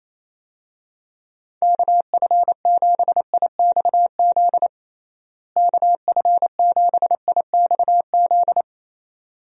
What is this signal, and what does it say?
Morse code, heard as a keyed tone, about 28 wpm, 700 Hz. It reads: KF7IXZ KF7IXZ
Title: USA Call Signs - 6 Characters - Repeated 2x - 28wpm